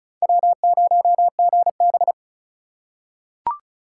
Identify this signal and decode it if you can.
Morse code, heard as a keyed tone, about 36 wpm, 700 Hz. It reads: W0GB
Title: USA Call Signs - 3x Character Spacing - 35wpm